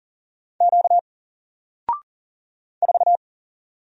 Morse code, heard as a keyed tone, about 40 wpm, 700 Hz. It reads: Q 4
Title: Single Letter-Number Alt 40wpm